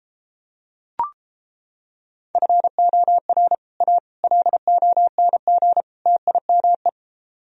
Morse code, heard as a keyed tone, about 29 wpm, 700 Hz. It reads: FOR A LONG TIME